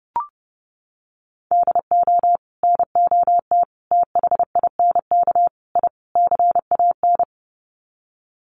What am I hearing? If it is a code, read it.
Morse code, heard as a keyed tone, about 30 wpm, 700 Hz. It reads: DO NOT THINK I CAN
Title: Sets of 5 Words Encore2 30wpm